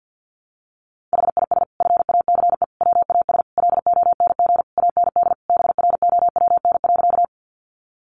Morse code, heard as a keyed tone, about 49 wpm, 700 Hz. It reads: HIS FACE WAS LONG AND BROWN.
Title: Sentences from Top 1200 Words 50wpm